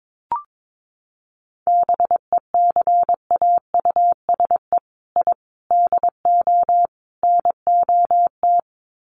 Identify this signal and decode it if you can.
Morse code, heard as a keyed tone, about 22 wpm, 700 Hz. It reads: BECAUSE I DO NOT
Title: Sets of 4 Words 22wpm